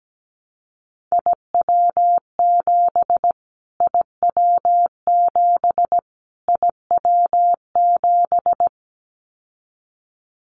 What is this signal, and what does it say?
Morse code, heard as a keyed tone, about 17 wpm, 700 Hz. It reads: IW7 IW7 IW7
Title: Callsign Prefixes for Top 25 DXCC - Repeated 3x - 17wpm